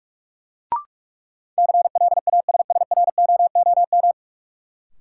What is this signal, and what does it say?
Morse code, heard as a keyed tone, about 46 wpm, 700 Hz. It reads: CLASSROOM
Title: Sentences from Top 1500 Words - New Word Drill Alt - 45wpm